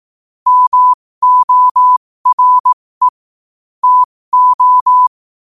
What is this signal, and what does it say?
Morse code, heard as a keyed tone, about 18 wpm, spaced as for 15 wpm, 1000 Hz. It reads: MORE TO
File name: qotdfiles / 15wpm